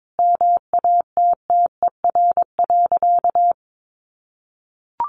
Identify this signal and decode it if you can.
Morse code, heard as a keyed tone, about 22 wpm, 700 Hz. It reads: MATTER.
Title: Sentences from Top 1300 Words 22wpm